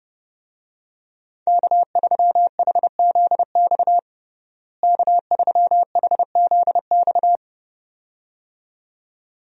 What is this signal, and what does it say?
Morse code, heard as a keyed tone, about 30 wpm, 700 Hz. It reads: K3HZX K3HZX
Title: USA Call Signs - 5 Characters - Repeated 2x - 30wpm